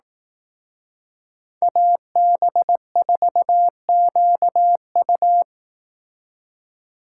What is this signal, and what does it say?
Morse code, heard as a keyed tone, about 18 wpm, 700 Hz. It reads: AB4QU
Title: Speed Racing - USA 2x2 Call Signs 15wpm